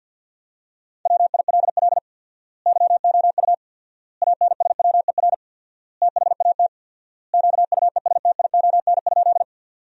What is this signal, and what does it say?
Morse code, heard as a keyed tone, about 50 wpm, 700 Hz. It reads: WILL YOU ANSWER THAT QUESTION?
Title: 4x Word-Spacing - Sentences from Top 400 Words - 50wpm